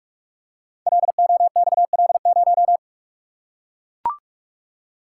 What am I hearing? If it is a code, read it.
Morse code, heard as a keyed tone, about 44 wpm, 700 Hz. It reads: LOXL0
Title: Code Groups - 5 Letter:Numbers - 45wpm